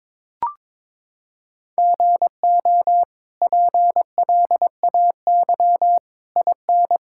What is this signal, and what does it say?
Morse code, heard as a keyed tone, about 22 wpm, 700 Hz. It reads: GO PLAY IN
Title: Sentences from Top 1000 Words - Review Part 5 - 22wpm